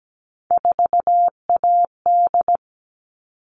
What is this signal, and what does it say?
Morse code, heard as a keyed tone, about 17 wpm, 700 Hz. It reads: 4AD
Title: All Cut Numbers - 3-Digit 17wpm